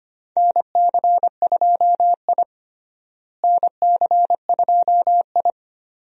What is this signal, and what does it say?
Morse code, heard as a keyed tone, about 25 wpm, 700 Hz. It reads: NC2I NC2I
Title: USA Call Signs - 4 Characters - Repeated 2x - 25wpm